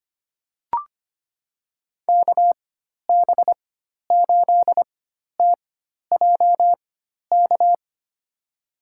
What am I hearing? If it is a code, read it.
Morse code, heard as a keyed tone, about 25 wpm, spaced as for 12 wpm, 700 Hz. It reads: KB8TJK
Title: USA Call Signs - 6 Characters - 4x Character Spacing - 25wpm